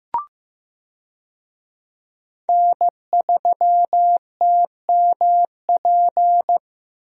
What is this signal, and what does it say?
Morse code, heard as a keyed tone, about 15 wpm, 700 Hz. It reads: N3TMP
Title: USA Call Signs - 5 Characters - Repeated 2x - 15wpm